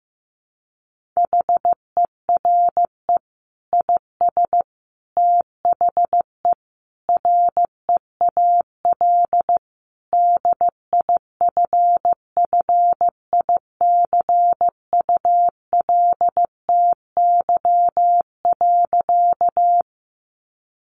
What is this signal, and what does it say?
Morse code, heard as a keyed tone, about 15 wpm, 700 Hz. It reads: HERE IS THE REAL DIFFICULTY.
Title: Sentences from Top 1500 Words 15wpm